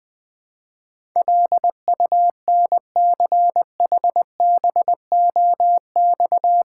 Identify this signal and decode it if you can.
Morse code, heard as a keyed tone, about 20 wpm, 700 Hz. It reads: LUNCHBOX
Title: Easy Words with Hints - 20wpm